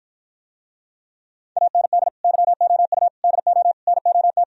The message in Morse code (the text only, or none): AND YOU DO NOT